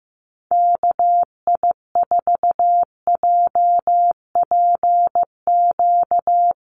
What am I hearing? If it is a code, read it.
Morse code, heard as a keyed tone, about 15 wpm, 700 Hz. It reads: KI4JPQ
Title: USA Call Signs Repeated 3x 15wpm